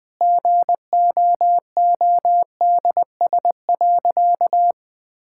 Morse code, heard as a keyed tone, about 20 wpm, 700 Hz. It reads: GOODS.
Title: Sentences from Top 1000 Words - Review Part 6 - 20wpm